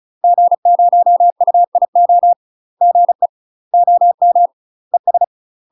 Morse code, heard as a keyed tone, about 34 wpm, 700 Hz. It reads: G0UIO GE OM ES
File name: UK MASTER QSOs TRAINING / 35WPM / UK MASTER QSOs 0004